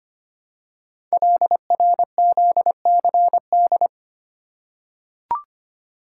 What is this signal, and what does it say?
Morse code, heard as a keyed tone, about 25 wpm, 700 Hz. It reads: LRZCD
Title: Code Groups - 5 Letters - 25wpm